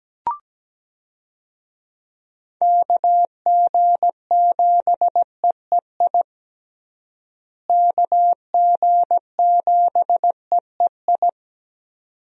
Morse code, heard as a keyed tone, about 17 wpm, 700 Hz. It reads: KG7EEI KG7EEI
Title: USA Call Signs - 6 Characters - Repeated 2x - 17wpm